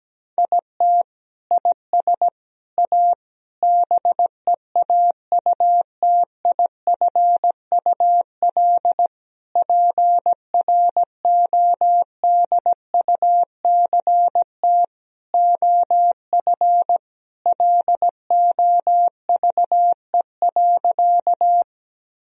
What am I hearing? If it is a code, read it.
Morse code, heard as a keyed tone, about 17 wpm, 700 Hz. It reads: IT IS A BEAUTIFUL PRODUCT OF LOVE.
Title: Sentences from Top 1000 Words - Review Part 2 - 17wpm